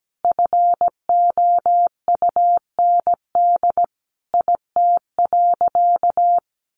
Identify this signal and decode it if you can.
Morse code, heard as a keyed tone, about 17 wpm, 700 Hz. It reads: FOUND IT.
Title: Sentences from Top 300 Words 17wpm